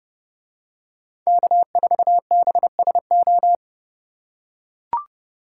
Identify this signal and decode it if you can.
Morse code, heard as a keyed tone, about 29 wpm, 700 Hz. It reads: K4BSO